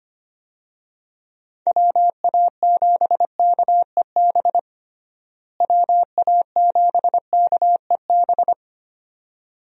Morse code, heard as a keyed tone, about 25 wpm, 700 Hz. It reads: WA7KEB WA7KEB
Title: USA Call Signs - 6 Characters - Repeated 2x - 25wpm